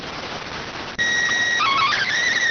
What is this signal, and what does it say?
SSTV (slow-scan television), PD 290.
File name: findings / S ss-portrait.gen-19 rx